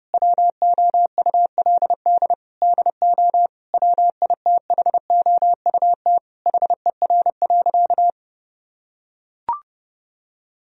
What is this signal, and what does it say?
Morse code, heard as a keyed tone, about 32 wpm, 700 Hz. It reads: WOULD DO WITHOUT HER.
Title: Sentences from Top 1000 Words - Review Part 3 - 30wpm